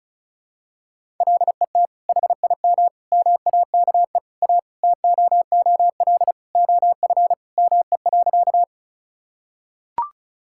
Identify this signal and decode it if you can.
Morse code, heard as a keyed tone, about 34 wpm, 700 Hz. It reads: LET HIM MAKE A TOOL OF ME.
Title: Sentences from Top 1000 Words - Review Part 6 - 35wpm